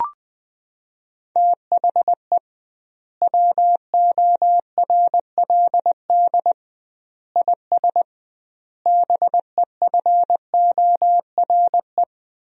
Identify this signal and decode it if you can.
Morse code, heard as a keyed tone, about 20 wpm, 700 Hz. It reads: THE WORLD IS BEFORE
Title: 2x Word-Spacing - Sentences from Top 200 Words - 20wpm